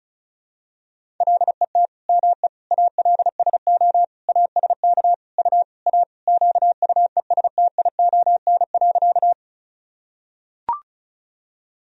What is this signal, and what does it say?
Morse code, heard as a keyed tone, about 36 wpm, 700 Hz. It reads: LET ME ALSO ASK U A QUESTION.
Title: Sentences from Top 400 Words and Abbreviations 35wpm